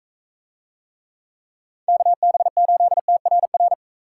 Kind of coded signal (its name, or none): Morse code